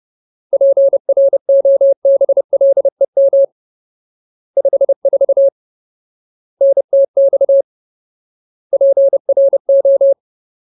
Morse code, heard as a keyed tone, about 30 wpm, 550 Hz. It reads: PROBLEM 54 NTX PRO
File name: Ultimate Word List / 30WPM Ultimate